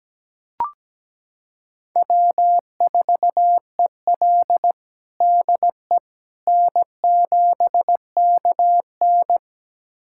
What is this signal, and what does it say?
Morse code, heard as a keyed tone, about 17 wpm, 700 Hz. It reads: W4EL DE N7KN
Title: USA Callsign DE Callsign - 4 Characters - 17wpm